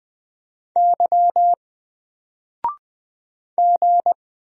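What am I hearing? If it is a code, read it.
Morse code, heard as a keyed tone, about 20 wpm, 700 Hz. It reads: Y G
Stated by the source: Single letter and number Morse drill at 20 wpm, from N4LCD.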